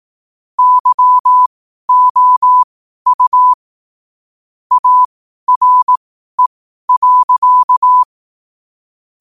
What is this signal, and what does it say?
Morse code, heard as a keyed tone, about 18 wpm, spaced as for 12 wpm, 1000 Hz. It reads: YOU ARE.
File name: qotdfiles / 12wpm